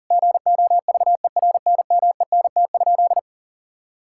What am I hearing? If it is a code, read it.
Morse code, heard as a keyed tone, about 40 wpm, 700 Hz. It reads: GOVERNMENT?
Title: Sentences from Top 1600 Words 40wpm